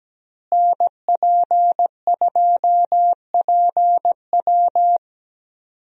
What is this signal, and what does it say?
Morse code, heard as a keyed tone, about 17 wpm, 700 Hz. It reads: NP2PW
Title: USA Call Signs - 5 Characters - Repeated 3x - 17wpm